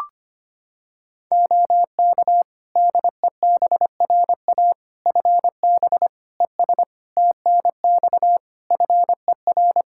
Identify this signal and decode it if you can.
Morse code, heard as a keyed tone, about 25 wpm, 700 Hz. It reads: OK DEBRA FB ES TNX FER
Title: Practice QSO - Exchange 2 - Info - 25wpm